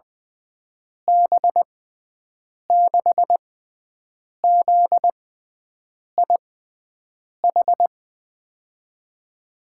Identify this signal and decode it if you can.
Morse code, heard as a keyed tone, about 20 wpm, spaced as for 7 wpm, 700 Hz. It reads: B6ZIH
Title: USA Call Signs - 6 Characters - 6x Character Spacing - 20wpm